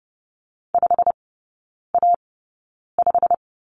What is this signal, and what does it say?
Morse code, heard as a keyed tone, about 30 wpm, spaced as for 9 wpm, 700 Hz. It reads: 5A5
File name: course / 30:10wpm / 087 Lesson 15e - Number 5 - Three Characters 30wpm